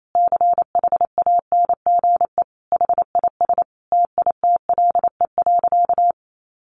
Morse code, heard as a keyed tone, about 28 wpm, 700 Hz. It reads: CHANGE HIS TITLE.
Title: Sentences from Top 1200 Words 28wpm